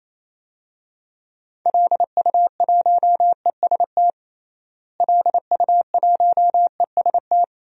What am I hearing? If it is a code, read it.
Morse code, heard as a keyed tone, about 28 wpm, 700 Hz. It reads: LU1EST LU1EST
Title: DXCC-100 Call Signs - 6 Characters - Repeated 3x - 28wpm